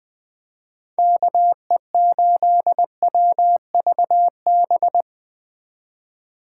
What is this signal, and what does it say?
Morse code, heard as a keyed tone, about 20 wpm, 700 Hz. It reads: KE8WVB
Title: 2023 POTA Activator Call Signs - 20wpm